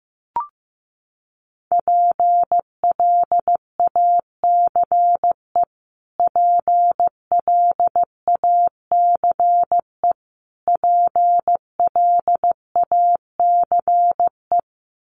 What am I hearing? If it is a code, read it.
Morse code, heard as a keyed tone, about 15 wpm, 700 Hz. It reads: PLACE PLACE PLACE
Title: Top 500 Words Repeated 15wpm